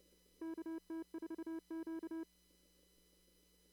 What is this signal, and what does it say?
Morse code, heard as a keyed tone, about 30 wpm, 330 Hz. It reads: KT4Q
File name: CW / kt5j wpx cw-op-k5pi